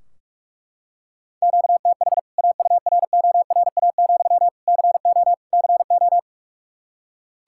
Morse code, heard as a keyed tone, about 42 wpm, 700 Hz. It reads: QTH AURORA, CO CO
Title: K1USN SST QSO Practice 45wpm